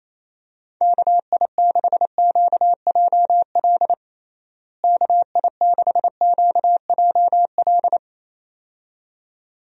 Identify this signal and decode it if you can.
Morse code, heard as a keyed tone, about 28 wpm, 700 Hz. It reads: KI6QJL KI6QJL
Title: USA Call Signs - 6 Characters - Repeated 2x - 28wpm